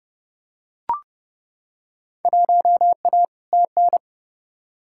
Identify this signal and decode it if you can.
Morse code, heard as a keyed tone, about 30 wpm, 700 Hz. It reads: 1A TN
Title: Field Day Exchanges - 30wpm